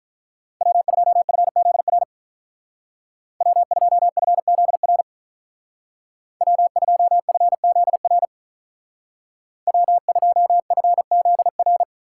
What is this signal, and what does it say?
Morse code, heard as a keyed tone, about 43 wpm, 700 Hz. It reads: W2FZR W2FZR W2FZR W2FZR
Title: Speed Racing - USA 1x3 Call Signs 35wpm